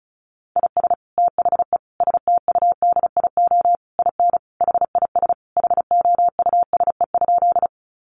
Morse code, heard as a keyed tone, about 34 wpm, 700 Hz. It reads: IS THE STUDIO IN HIS HOUSE?
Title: Sentences from Top 1200 Words 35wpm